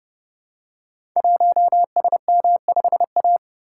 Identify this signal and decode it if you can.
Morse code, heard as a keyed tone, about 30 wpm, 700 Hz. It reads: 1SM5A
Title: Code Groups - 5 Letter:Numbers - 30wpm